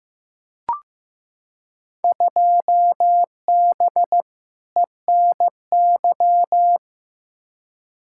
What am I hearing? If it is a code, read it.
Morse code, heard as a keyed tone, about 15 wpm, 700 Hz. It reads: 2B ENY
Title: Field Day Exchanges Encore - 15wpm